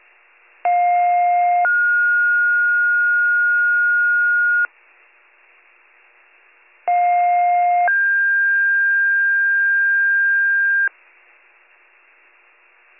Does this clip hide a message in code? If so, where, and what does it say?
Yes: two-tone page from 0.6 to 10.9 s, heard as a one-second tone, then a three-second tone, about 710 then 1430 Hz, about 710 then 1600 Hz.